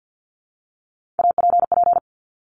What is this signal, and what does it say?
Morse code, heard as a keyed tone, about 50 wpm, 700 Hz. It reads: APP
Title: Top Three Letter Combinations in Words 50wpm